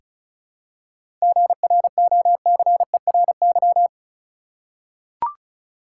Morse code, heard as a keyed tone, about 36 wpm, 700 Hz. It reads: GROCERY
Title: Seven Letter Words - 35wpm